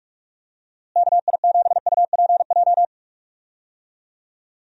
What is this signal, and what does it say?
Morse code, heard as a keyed tone, about 46 wpm, 700 Hz. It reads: KI7UPJ
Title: USA Call Signs - 6 Characters - Repeated 2x - 45wpm